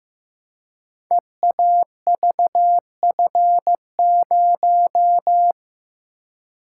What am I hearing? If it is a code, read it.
Morse code, heard as a keyed tone, about 15 wpm, 700 Hz. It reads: EAVF0
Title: Code Groups - 5 Letter:Numbers - 15wpm